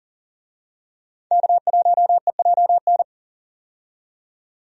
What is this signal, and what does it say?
Morse code, heard as a keyed tone, about 40 wpm, 700 Hz. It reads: K1EJN
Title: USA Call Signs - 5 Characters - 40wpm